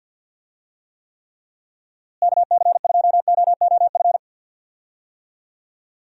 Morse code, heard as a keyed tone, about 52 wpm, 700 Hz. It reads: KC2OOF
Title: USA Call Signs - 6 Characters - Repeated 3x - 50wpm